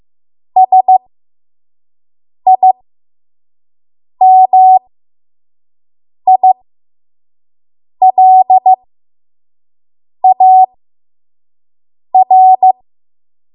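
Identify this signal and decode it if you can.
Morse code, heard as a keyed tone, about 15 wpm, spaced as for 5 wpm, 750 Hz. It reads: SIMILAR
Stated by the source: 5 WPM Morse code practice.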